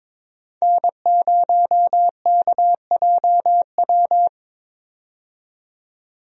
Morse code, heard as a keyed tone, about 22 wpm, 700 Hz. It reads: N0KJW
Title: USA Call Signs - 5 Characters - 22wpm